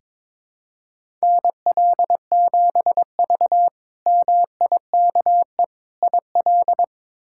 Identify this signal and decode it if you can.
Morse code, heard as a keyed tone, about 22 wpm, 700 Hz. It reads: NL7V MIKE IL
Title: K1USN SST QSO Practice - 22wpm